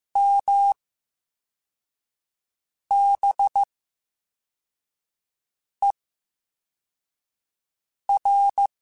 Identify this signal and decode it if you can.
Morse code, heard as a keyed tone, about 15 wpm, spaced as for 4 wpm, 800 Hz. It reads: MBER
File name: Special INGs-mp3s / Special INGs-mp3s / VerySlowToFaster Ing @13